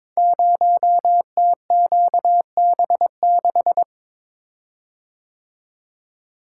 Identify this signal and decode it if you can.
Morse code, heard as a keyed tone, about 22 wpm, 700 Hz. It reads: 0TQB6
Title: Code Groups - 5 Letter:Numbers - 22wpm